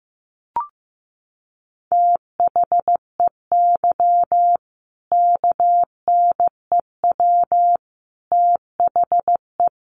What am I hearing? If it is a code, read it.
Morse code, heard as a keyed tone, about 14 wpm, 700 Hz. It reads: THEY KNEW THE
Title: Sentences from Top 1400 Words 15wpm